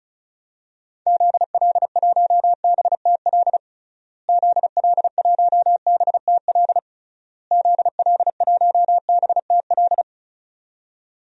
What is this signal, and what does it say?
Morse code, heard as a keyed tone, about 36 wpm, 700 Hz. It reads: ZL1BTL ZL1BTL ZL1BTL
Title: DXCC-100 Call Signs - 6 Characters - Repeated 3x - 35wpm